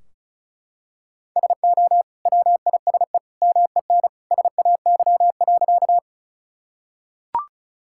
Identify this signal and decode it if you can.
Morse code, heard as a keyed tone, about 36 wpm, 700 Hz. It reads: SO WISE MEN SAY.
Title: Sentences from Top 1600 Words 35wpm